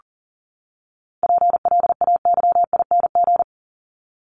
Morse code, heard as a keyed tone, about 40 wpm, 700 Hz. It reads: PLAYING
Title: ING Suffix - Top 100 Words with Base Words 40wpm